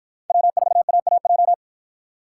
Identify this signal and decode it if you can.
Morse code, heard as a keyed tone, about 53 wpm, 700 Hz. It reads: W4AAJ